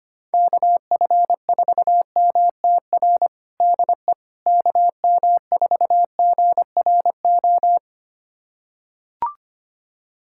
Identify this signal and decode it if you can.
Morse code, heard as a keyed tone, about 25 wpm, 700 Hz. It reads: KF4MTR DE KM4GRO
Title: USA Callsign DE Callsign 25wpm